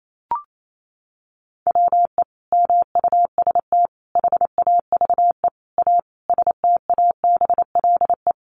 Morse code, heard as a keyed tone, about 28 wpm, 700 Hz. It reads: WE MUST HAVE A STABLE
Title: Sentences from Top 1500 Words 28wpm